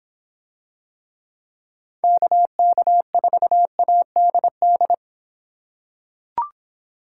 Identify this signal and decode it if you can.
Morse code, heard as a keyed tone, about 24 wpm, 700 Hz. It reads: KK4ADD